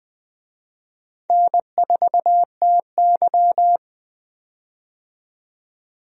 Morse code, heard as a keyed tone, about 20 wpm, 700 Hz. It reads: N4TY
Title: USA Call Signs - 4 Characters - 20wpm